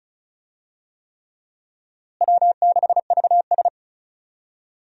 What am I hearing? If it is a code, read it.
Morse code, heard as a keyed tone, about 36 wpm, 700 Hz. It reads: W6VS
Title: USA Call Signs - 4 Characters - Repeated 3x - 35wpm